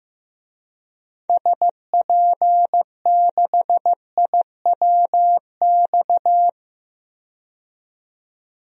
Morse code, heard as a keyed tone, about 15 wpm, 700 Hz. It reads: SP6IWX
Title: DXCC-100 Call Signs - 6 Characters - 15wpm